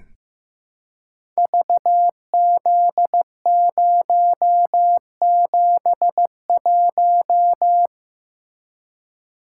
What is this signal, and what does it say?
Morse code, heard as a keyed tone, about 15 wpm, 700 Hz. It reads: VZ071
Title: Code Groups - 5 Letter:Numbers - 15wpm